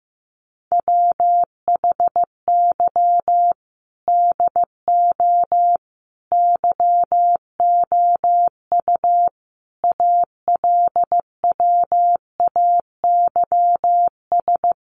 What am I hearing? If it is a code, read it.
Morse code, heard as a keyed tone, about 15 wpm, 700 Hz. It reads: WHY DO YOU ALWAYS
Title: Sentences from Top 1300 Words 15wpm